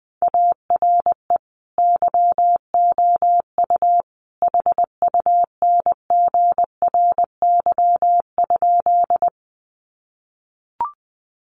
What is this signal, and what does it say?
Morse code, heard as a keyed tone, about 20 wpm, 700 Hz. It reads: ARE YOU HUNGRY?
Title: Sentences from Top 1400 Words 20wpm